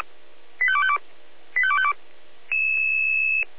Five-tone selective call (ZVEI), heard as quick runs of five tones, tones of about 70 ms. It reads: 85252 85252